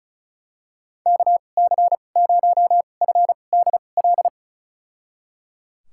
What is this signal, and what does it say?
Morse code, heard as a keyed tone, about 36 wpm, spaced as for 25 wpm, 700 Hz. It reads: KC0FDL
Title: USA Call Signs - 6 Characters - 2x Character Spacing - 35wpm